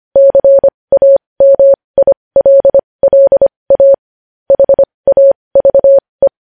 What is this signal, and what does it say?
Morse code, heard as a keyed tone, about 25 wpm, spaced as for 20 wpm, 550 Hz. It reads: CAMILLA HAVE
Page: News Headlines in Morse Code at 20 WPM